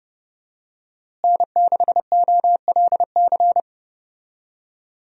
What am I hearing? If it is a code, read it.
Morse code, heard as a keyed tone, about 30 wpm, 700 Hz. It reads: N6OLC
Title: USA Call Signs - 5 Characters - Repeated 2x - 30wpm